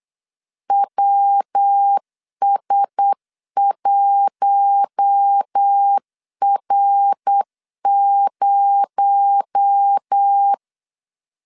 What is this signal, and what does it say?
Morse code, heard as a keyed tone, about 8 wpm, 800 Hz. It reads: WS1R0
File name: 8wpm